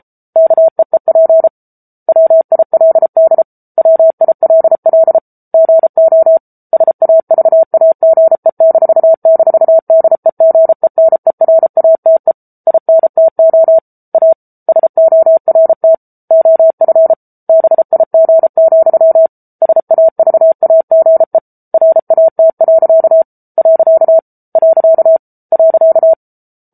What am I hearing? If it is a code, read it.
Morse code, heard as a keyed tone, about 34 wpm, 650 Hz. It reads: KEEP WILD WILL GO SAVAGE--DEGENERATE INTO A SORT OF BIG, SAVAGE RAT. .